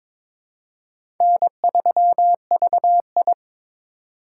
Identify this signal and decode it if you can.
Morse code, heard as a keyed tone, about 22 wpm, 700 Hz. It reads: N3VI
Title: USA Call Signs - 4 Characters - 3x Character Spacing - 22wpm